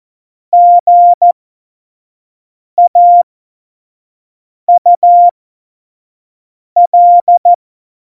Morse code, heard as a keyed tone, about 14 wpm, spaced as for 5 wpm, 700 Hz. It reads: GAUL
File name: Morsekurs / Witze / Witz 11 August 2020, 25 F70